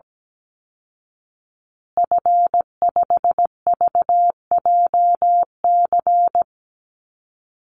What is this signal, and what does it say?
Morse code, heard as a keyed tone, about 17 wpm, 700 Hz. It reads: F5VJC
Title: Call Signs Repeated 3x - 17wpm